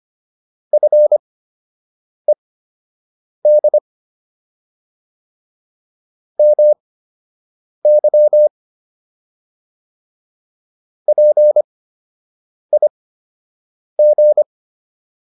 Morse code, heard as a keyed tone, about 25 wpm, spaced as for 7 wpm, 600 Hz. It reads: FED MY PIG